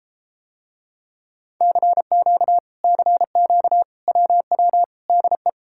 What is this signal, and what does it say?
Morse code, heard as a keyed tone, about 33 wpm, 700 Hz. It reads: CQ CQ WW DE